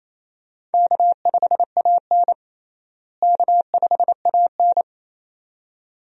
Morse code, heard as a keyed tone, about 28 wpm, 700 Hz. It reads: K5AN K5AN
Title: USA Call Signs - 4 Characters - Repeated 2x - 28wpm